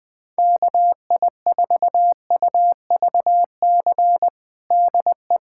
Morse code, heard as a keyed tone, about 20 wpm, 700 Hz. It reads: KI4UVC DE
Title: USA Callsign DE Callsign - 6 Characters - 20wpm